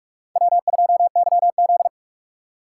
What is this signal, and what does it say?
Morse code, heard as a keyed tone, about 42 wpm, 700 Hz. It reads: W2YZ